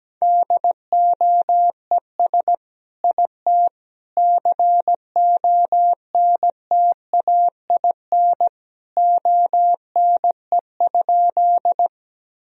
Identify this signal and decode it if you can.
Morse code, heard as a keyed tone, about 17 wpm, 700 Hz. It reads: DOES IT CONTAIN ONE?
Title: Sentences from Top 1000 Words - Review Part 2 - 17wpm